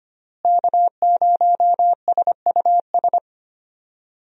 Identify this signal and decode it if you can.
Morse code, heard as a keyed tone, about 25 wpm, 700 Hz. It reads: K0SUS
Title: USA Call Signs - 5 Characters - Repeated 2x - 25wpm